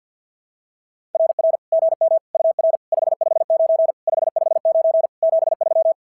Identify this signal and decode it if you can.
Morse code, heard as a keyed tone, about 46 wpm, 660 Hz. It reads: RR GM UR 559 559 73
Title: SOTA QSO Practice 50wpm